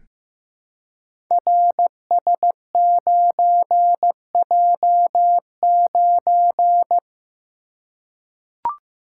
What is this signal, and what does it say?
Morse code, heard as a keyed tone, about 15 wpm, 700 Hz. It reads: RS9J9
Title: Code Groups - 5 Letter:Numbers - 15wpm